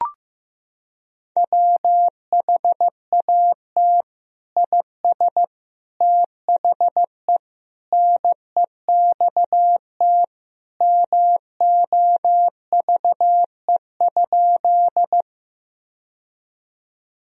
Morse code, heard as a keyed tone, about 15 wpm, 700 Hz. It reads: WHAT IS THE NEXT MOVE?
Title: Sentences from Top 1000 Words - Review Part 1 - 15wpm